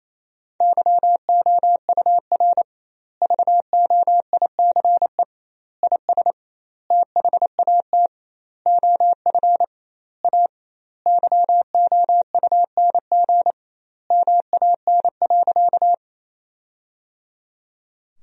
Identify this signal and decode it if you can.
Morse code, heard as a keyed tone, about 28 wpm, 700 Hz. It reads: YOUR VOICE IS THAT OF A YOUNG MAN.
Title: 2x Word-Spacing - Sentences from Top 500 Words - 28wpm